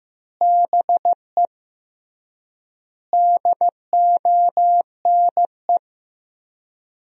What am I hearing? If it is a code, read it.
Morse code, heard as a keyed tone, about 15 wpm, 700 Hz. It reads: BE DONE
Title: 3x Word-Spacing - Sentences from Top 500 Words - 15wpm